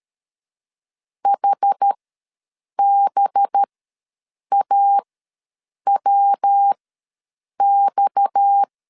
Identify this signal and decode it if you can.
Morse code, heard as a keyed tone, about 13 wpm, spaced as for 7 wpm, 800 Hz. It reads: HBAWX
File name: LEZIONI / 6 PKX0 / 6 PKX ws 12wpm 3106